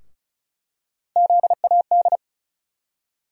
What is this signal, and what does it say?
Morse code, heard as a keyed tone, about 34 wpm, 700 Hz. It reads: ZAD
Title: USA Call Sign Postfixes - 3 Letters 35wpm